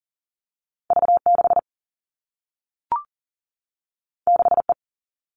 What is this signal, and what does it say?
Morse code, heard as a keyed tone, about 40 wpm, 700 Hz. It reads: V 6 6 E
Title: Two Challenging Letter-Number Combinations 40wpm